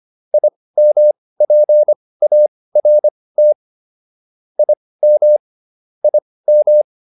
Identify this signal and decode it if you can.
Morse code, heard as a keyed tone, about 25 wpm, spaced as for 18 wpm, 600 Hz. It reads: IMPART IM IM